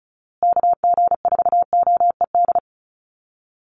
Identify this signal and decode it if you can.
Morse code, heard as a keyed tone, about 34 wpm, 700 Hz. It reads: KG4OED
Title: USA Call Signs Repeated 3x 35wpm